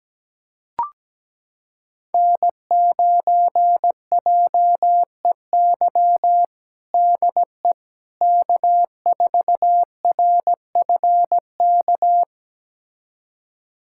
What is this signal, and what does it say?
Morse code, heard as a keyed tone, about 17 wpm, 700 Hz. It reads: N9JEY DE K4RFK
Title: USA Callsign DE Callsign - 5 Characters - 17wpm